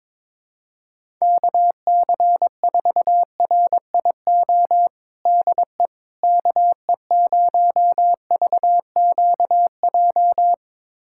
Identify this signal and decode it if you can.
Morse code, heard as a keyed tone, about 22 wpm, 700 Hz. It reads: KC4RIO DE KE0VQJ
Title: USA Callsign DE Callsign 22wpm